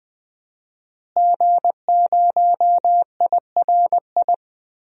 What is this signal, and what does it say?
Morse code, heard as a keyed tone, about 20 wpm, 700 Hz. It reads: G0IRI
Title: DXCC-100 Call Signs - 5 Characters - Repeated 3x - 20wpm